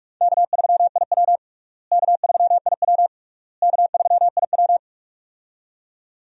Morse code, heard as a keyed tone, about 42 wpm, 700 Hz. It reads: K3IW K3IW K3IW
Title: USA Call Signs - 4 Characters - Repeated 3x - 45wpm